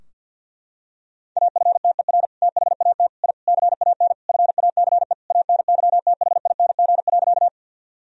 Morse code, heard as a keyed tone, about 53 wpm, 700 Hz. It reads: AFTER THAT I CAN FACE ANYTHING.
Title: Sentences from Top 1000 Words - Review Part 2 - 50wpm